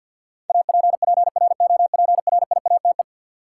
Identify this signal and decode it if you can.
Morse code, heard as a keyed tone, about 53 wpm, 700 Hz. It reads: APPROPRIATE
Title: Sentences from Top 900 Words - New Word Drill - 50wpm